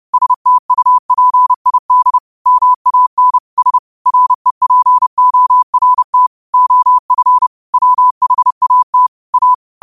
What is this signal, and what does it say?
Morse code, heard as a keyed tone, about 30 wpm, 1000 Hz. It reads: STUPID MAN S REPORT OF WHAT A
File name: qotdfiles / 30wpm